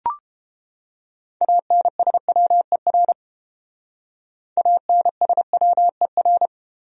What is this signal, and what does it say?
Morse code, heard as a keyed tone, about 28 wpm, 700 Hz. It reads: ANSWER ANSWER